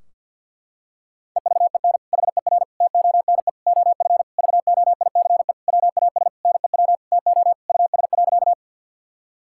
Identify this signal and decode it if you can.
Morse code, heard as a keyed tone, about 50 wpm, 700 Hz. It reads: EVEN HER TONE OF VOICE WAS NEW TO US.
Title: Sentences from Top 1600 Words 50wpm